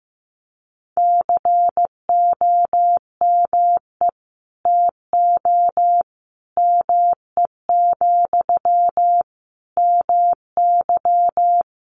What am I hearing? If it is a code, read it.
Morse code, heard as a keyed tone, about 15 wpm, 700 Hz. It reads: COME TO ME, MY
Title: Sentences from Top 1500 Words 15wpm